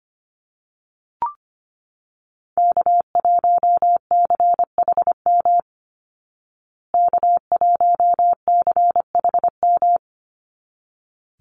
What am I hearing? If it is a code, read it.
Morse code, heard as a keyed tone, about 25 wpm, 700 Hz. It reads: K1CHM K1CHM
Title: Call Signs Repeated 3x - 25wpm